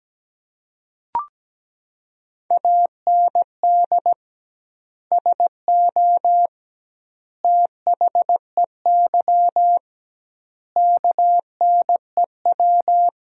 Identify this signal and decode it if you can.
Morse code, heard as a keyed tone, about 17 wpm, 700 Hz. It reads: AND SO THEY KNEW
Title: 2x Word-Spacing - Sentences from Top 400 Words - 17wpm